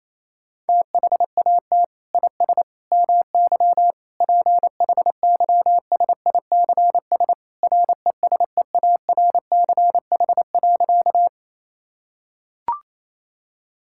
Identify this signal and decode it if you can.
Morse code, heard as a keyed tone, about 28 wpm, 700 Hz. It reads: THAT IS MY PHYSICS RESEARCH.
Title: Sentences from Top 1000 Words - Review Part 5 - 28wpm